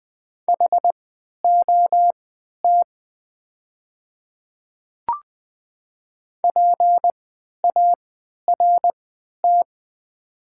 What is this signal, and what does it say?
Morse code, heard as a keyed tone, about 20 wpm, spaced as for 11 wpm, 700 Hz. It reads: HOT PART
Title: 3x Character Spacing - Top 200 Words - 20wpm